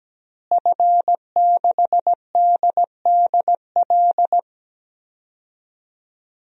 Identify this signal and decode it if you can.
Morse code, heard as a keyed tone, about 17 wpm, 700 Hz. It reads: F6DDL
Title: DXCC-100 Call Signs - 5 Characters - Repeated 3x - 17wpm